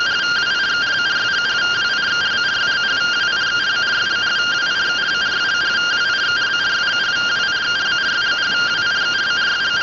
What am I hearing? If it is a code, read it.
Two-tone teleprinter signal, about 170 Hz shift.